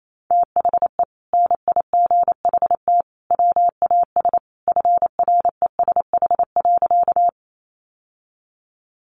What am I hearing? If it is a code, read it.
Morse code, heard as a keyed tone, about 28 wpm, 700 Hz. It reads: THE NIGHT WAS FRESH.
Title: Sentences from Top 1200 Words 28wpm